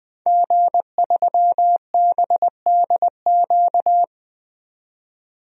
Morse code, heard as a keyed tone, about 20 wpm, 700 Hz. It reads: G3BDQ
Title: DXCC-100 Call Signs - 5 Characters - 20wpm